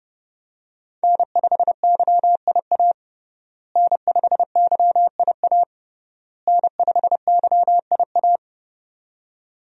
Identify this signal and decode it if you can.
Morse code, heard as a keyed tone, about 30 wpm, 700 Hz. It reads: N5YIA N5YIA N5YIA
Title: USA Call Signs - 5 Characters - Repeated 3x - 30wpm